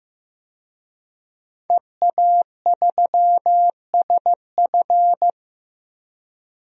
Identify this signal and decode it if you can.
Morse code, heard as a keyed tone, about 15 wpm, 700 Hz. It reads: EA3SF
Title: DXCC-100 Call Signs - 5 Characters - 15wpm